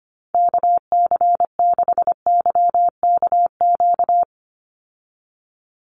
Morse code, heard as a keyed tone, about 25 wpm, 700 Hz. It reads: KC6YKQ
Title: USA Call Signs 25wpm